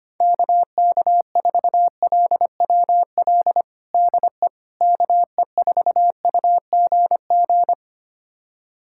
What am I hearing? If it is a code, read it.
Morse code, heard as a keyed tone, about 25 wpm, 700 Hz. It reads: KK4LWL DE KE4UGG
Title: USA Callsign DE Callsign - 6 Characters - 25wpm